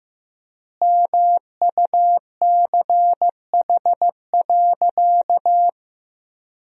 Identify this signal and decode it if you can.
Morse code, heard as a keyed tone, about 15 wpm, 700 Hz. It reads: MUCH.
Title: 3x Word-Spacing - Sentences from Top 500 Words - 15wpm